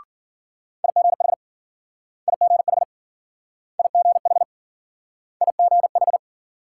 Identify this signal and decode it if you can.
Morse code, heard as a keyed tone, about 54 wpm, 700 Hz. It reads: IGH IGH IGH IGH